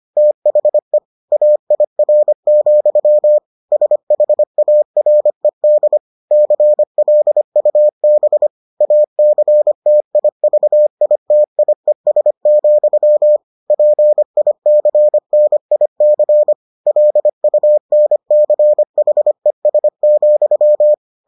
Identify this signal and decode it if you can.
Morse code, heard as a keyed tone, about 25 wpm, 600 Hz. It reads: THE AIR, SHARED CLUB ACTIVITIES, PICNIC LUNCHES,